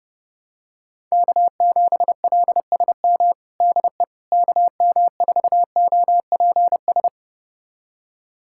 Morse code, heard as a keyed tone, about 32 wpm, 700 Hz. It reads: K7LSM DE KM4OPS
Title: USA Callsign DE Callsign 30wpm